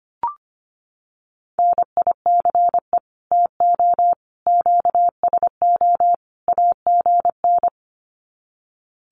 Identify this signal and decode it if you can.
Morse code, heard as a keyed tone, about 25 wpm, 700 Hz. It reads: NICE TO QSO AGN
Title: Common QSO Phrases 25wpm